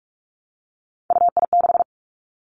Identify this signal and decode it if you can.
Morse code, heard as a keyed tone, about 46 wpm, 700 Hz. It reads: UI6
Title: Three Challenging Letter-Number Combinations 45wpm